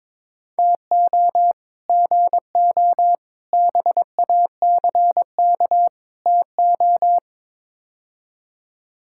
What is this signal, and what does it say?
Morse code, heard as a keyed tone, about 22 wpm, 700 Hz. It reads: TO GO BACK TO
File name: practice / 22wpm / Build A Phrase - 5-Word Phrases - 22wpm